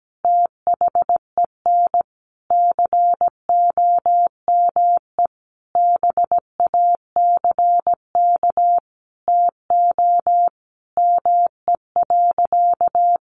Morse code, heard as a keyed tone, about 17 wpm, 700 Hz. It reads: THEN COME BACK TO ME.
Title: Sentences from Top 600 Words 17wpm